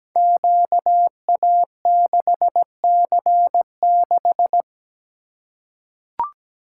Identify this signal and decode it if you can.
Morse code, heard as a keyed tone, about 17 wpm, 700 Hz. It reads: QA6C6
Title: Code Groups - 5 Letter:Numbers - 17wpm